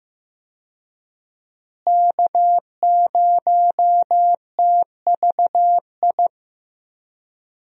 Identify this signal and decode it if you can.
Morse code, heard as a keyed tone, about 15 wpm, 700 Hz. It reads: K0TVI
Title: USA Call Signs - 5 Characters - Repeated 3x - 15wpm